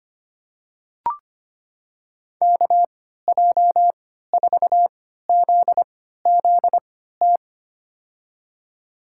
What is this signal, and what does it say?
Morse code, heard as a keyed tone, about 25 wpm, spaced as for 14 wpm, 700 Hz. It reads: KJ4ZZT
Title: USA Call Signs - 6 Characters - 3x Character Spacing - 25wpm